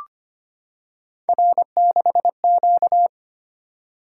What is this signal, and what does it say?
Morse code, heard as a keyed tone, about 25 wpm, 700 Hz. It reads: R6Q
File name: Three Letter-Number Combinations - 25wpm